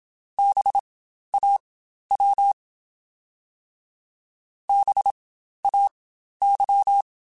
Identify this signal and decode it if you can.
Morse code, heard as a keyed tone, about 27 wpm, spaced as for 12 wpm, 800 Hz. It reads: BAW BAY
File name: Spaced Alphas / Spaced Alphas mp3s / Bs sp @25